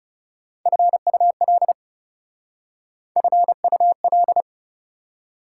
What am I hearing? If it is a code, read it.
Morse code, heard as a keyed tone, about 40 wpm, 690 Hz. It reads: FUL FUL